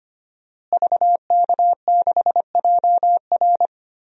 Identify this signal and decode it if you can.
Morse code, heard as a keyed tone, about 25 wpm, 700 Hz. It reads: VK6JR
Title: DXCC-100 Call Signs - 5 Characters - Repeated 2x - 25wpm